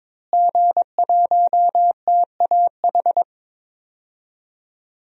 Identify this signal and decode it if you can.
Morse code, heard as a keyed tone, about 22 wpm, 700 Hz. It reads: G1TAH
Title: DXCC-100 Call Signs - 5 Characters - Repeated 3x - 22wpm